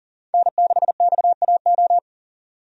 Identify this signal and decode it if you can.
Morse code, heard as a keyed tone, about 40 wpm, 700 Hz. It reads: N6XAO
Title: USA Call Signs - 5 Characters - 40wpm